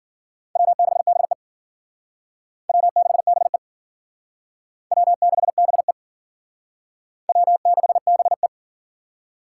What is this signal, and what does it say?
Morse code, heard as a keyed tone, about 50 wpm, 700 Hz. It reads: W6BE W6BE W6BE W6BE
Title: Speed Racing - USA 1x2 Call Signs 40wpm